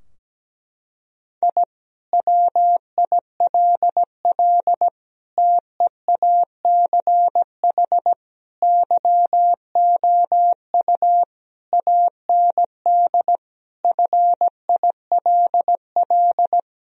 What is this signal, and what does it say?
Morse code, heard as a keyed tone, about 17 wpm, 700 Hz. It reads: I WILL TEACH YOU AND FILL
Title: Sentences from Top 1000 Words - Review Part 3 - 17wpm